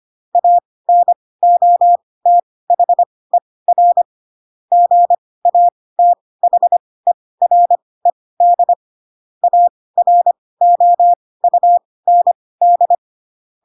Morse code, heard as a keyed tone, about 25 wpm, spaced as for 18 wpm, 700 Hz. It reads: ANOTHER GATHERED AROUND
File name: Pinocchio ch3 12 25 ik1hge